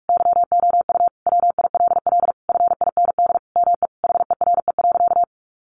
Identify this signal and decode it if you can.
Morse code, heard as a keyed tone, about 44 wpm, 700 Hz. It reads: YOU WILL FIND ME HERE.
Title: Sentences from Top 1200 Words 45wpm